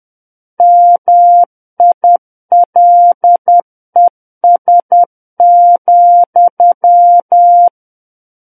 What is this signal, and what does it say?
Morse code, heard as a keyed tone, about 10 wpm, 700 Hz. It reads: MILES,
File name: Book One / 01-Chapter 1 - 10 WPM